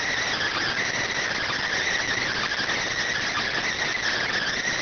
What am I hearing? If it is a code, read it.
Emergency Alert System (SAME) tones, heard as harsh buzzing bursts. It is heard from 0.7 to 4.1 s.